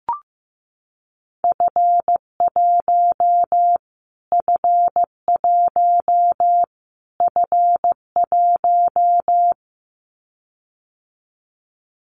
Morse code, heard as a keyed tone, about 14 wpm, 700 Hz. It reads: F1 F1 F1
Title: Callsign Prefixes for Top 25 DXCC - Repeated 3x - 15wpm